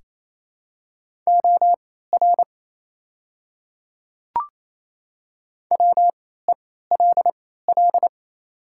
Morse code, heard as a keyed tone, about 28 wpm, spaced as for 16 wpm, 700 Hz. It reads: OR WELL
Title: 3x Character Spacing - Top 200 Words - 28wpm